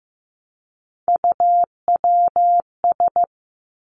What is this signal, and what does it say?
Morse code, heard as a keyed tone, about 15 wpm, 700 Hz. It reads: UWS
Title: Three Challenging Letter-Number Combinations 15wpm